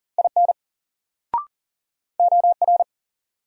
Morse code, heard as a keyed tone, about 40 wpm, 700 Hz. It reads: IN OR
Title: Sentences from Top 100 Words - New Word Drill - 40wpm